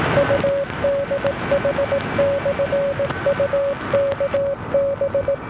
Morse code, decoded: UDH/UKB